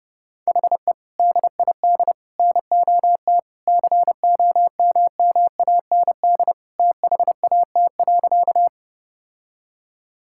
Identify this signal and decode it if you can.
Morse code, heard as a keyed tone, about 30 wpm, 700 Hz. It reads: HE DID NOT COMMAND THAT.
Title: Sentences from Top 1600 Words 30wpm